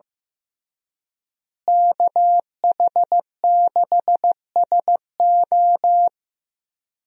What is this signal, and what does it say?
Morse code, heard as a keyed tone, about 15 wpm, 700 Hz. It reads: KH6SO
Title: USA Call Signs - 5 Characters - Repeated 3x - 15wpm